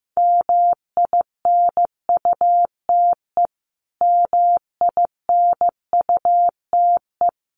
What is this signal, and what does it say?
Morse code, heard as a keyed tone, about 15 wpm, 700 Hz. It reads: MINUTE MINUTE
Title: Top 500 Words Repeated 15wpm